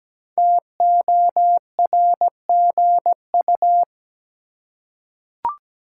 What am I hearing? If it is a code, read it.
Morse code, heard as a keyed tone, about 17 wpm, 700 Hz. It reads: TORGU
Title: Code Groups - 5 Letters - 17wpm